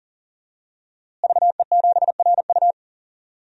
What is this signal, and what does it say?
Morse code, heard as a keyed tone, about 40 wpm, 700 Hz. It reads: VE7RU
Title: DXCC-100 Call Signs - 5 Characters - Repeated 2x - 40wpm